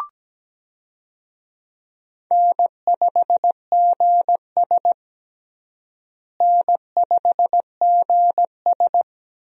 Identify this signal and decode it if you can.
Morse code, heard as a keyed tone, about 17 wpm, 700 Hz. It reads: N5GS N5GS
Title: USA Call Signs - 4 Characters - Repeated 2x - 17wpm